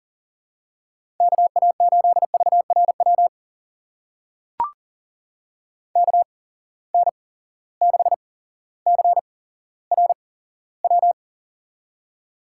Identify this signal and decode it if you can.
Morse code, heard as a keyed tone, about 38 wpm, spaced as for 17 wpm, 700 Hz. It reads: KA8VRW KN6CRW
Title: USA Call Signs - 6 Characters - 8x Character Spacing - 40wpm